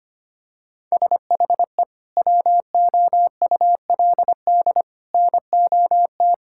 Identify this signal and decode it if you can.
Morse code, heard as a keyed tone, about 25 wpm, 700 Hz. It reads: SHE WOULD NOT
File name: practice / 25wpm / Sentences with 2-Word Phrases - 25wpm